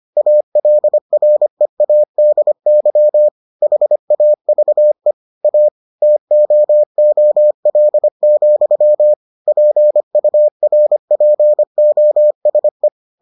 Morse code, heard as a keyed tone, about 25 wpm, 600 Hz. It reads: ALREADY HAVE A TOOL, PURPOSE